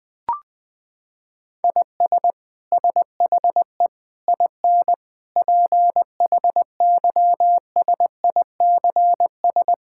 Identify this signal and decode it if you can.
Morse code, heard as a keyed tone, about 20 wpm, 700 Hz. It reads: IS SHE IN PHYSICS
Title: Sentences from Top 1000 Words - Review Part 5 - 20wpm